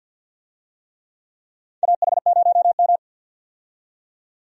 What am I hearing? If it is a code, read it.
Morse code, heard as a keyed tone, about 53 wpm, 700 Hz. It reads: AH0M
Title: USA Call Signs - 4 Characters - 50wpm